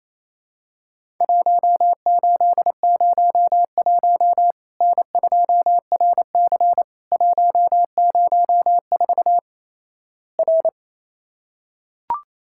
Morse code, heard as a keyed tone, about 28 wpm, 700 Hz. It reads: 1801 N2RC 104 R
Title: Worked All Europe DX Contest - QTC Exchange - 28wpm